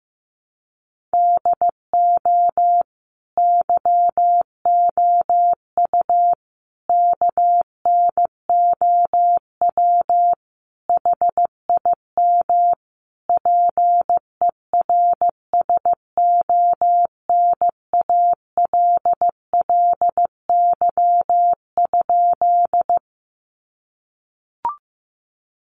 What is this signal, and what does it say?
Morse code, heard as a keyed tone, about 15 wpm, 700 Hz. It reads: DO YOU KNOW HIM PERSONALLY?
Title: Sentences from Top 1400 Words 15wpm